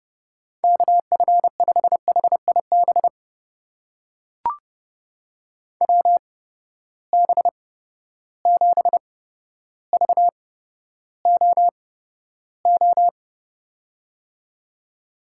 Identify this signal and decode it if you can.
Morse code, heard as a keyed tone, about 30 wpm, spaced as for 13 wpm, 700 Hz. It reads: KF5HIB WB7VOO
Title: USA Call Signs - 6 Characters - 8x Character Spacing - 30wpm